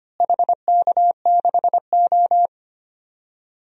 Morse code, heard as a keyed tone, about 25 wpm, 700 Hz. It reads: HK6O
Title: DXCC-100 Call Signs - 4 Characters - Repeated 3x - 25wpm